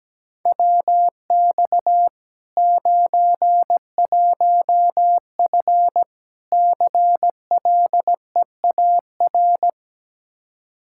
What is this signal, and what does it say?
Morse code, heard as a keyed tone, about 17 wpm, 700 Hz. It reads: WX 91F CLEAR
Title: Practice QSO - Exchange 2 - Info - 17wpm